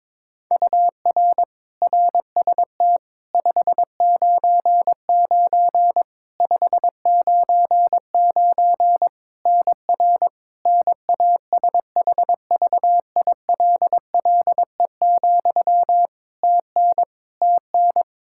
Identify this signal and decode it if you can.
Morse code, heard as a keyed tone, about 22 wpm, 700 Hz. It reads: UR RST 599 599 NR NASHVILLE, TN TN
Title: Practice QSO - Exchange 1 - Beginning - 22wpm